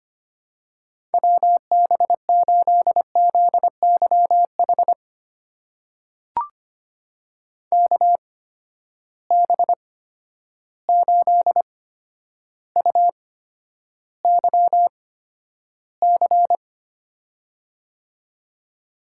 Morse code, heard as a keyed tone, about 25 wpm, spaced as for 11 wpm, 700 Hz. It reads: WB8ZYH KB8UYC
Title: USA Call Signs - 6 Characters - 8x Character Spacing - 25wpm